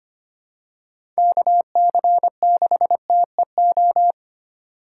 Morse code, heard as a keyed tone, about 25 wpm, 700 Hz. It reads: KC6TEO
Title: USA Call Signs - 4x Character Spacing - 25wpm